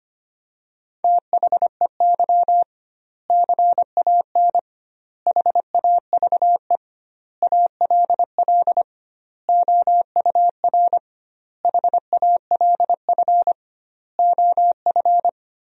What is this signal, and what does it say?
Morse code, heard as a keyed tone, about 25 wpm, 700 Hz. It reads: THEY CAN HAVE ALL OUR HALF OF
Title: 2x Word-Spacing - Sentences from Top 400 Words - 25wpm